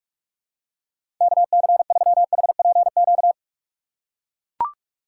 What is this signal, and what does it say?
Morse code, heard as a keyed tone, about 44 wpm, 700 Hz. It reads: KC3HPQ
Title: USA Call Signs - 6 Characters - 6x Character Spacing - 45wpm